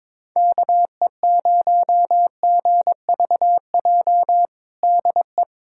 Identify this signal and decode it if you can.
Morse code, heard as a keyed tone, about 22 wpm, 700 Hz. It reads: KE0GVJ DE
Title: USA Callsign DE Callsign 22wpm